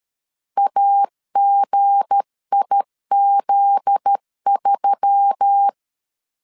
Morse code, heard as a keyed tone, about 13 wpm, 800 Hz. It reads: AGIZ3
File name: LEZIONI / 7 GF9Z / 7 GF9Z 12wpm 2147